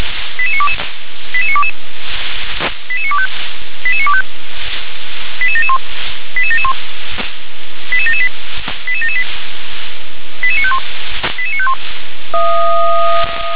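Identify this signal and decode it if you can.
Five-tone selective call (ZVEI), heard as quick runs of five tones, tones of about 70 ms. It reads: 88922 88922 88925 88925 88771 88771 88778 88778 88951 88951
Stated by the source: ZVEI five-tone selective call received on an SDR.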